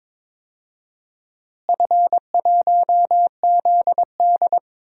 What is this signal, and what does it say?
Morse code, heard as a keyed tone, about 22 wpm, 700 Hz. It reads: F1ZD
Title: DXCC-100 Call Signs - 4 Characters - Repeated 2x - 22wpm